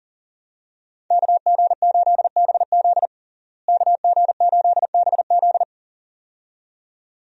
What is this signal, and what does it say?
Morse code, heard as a keyed tone, about 40 wpm, 700 Hz. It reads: KG8BZ KG8BZ
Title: USA Call Signs - 5 Characters - Repeated 2x - 40wpm